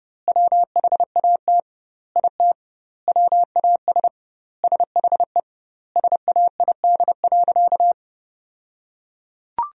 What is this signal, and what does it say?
Morse code, heard as a keyed tone, about 32 wpm, 700 Hz. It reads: WHAT IT WAS SHE SAID.
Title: 2x Word-Spacing - Sentences from Top 400 Words - 30wpm